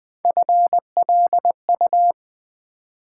Morse code, heard as a keyed tone, about 20 wpm, 700 Hz. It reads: FLU
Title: Build A Word - 1 Letter At A Time - 5-Letter Words - 20wpm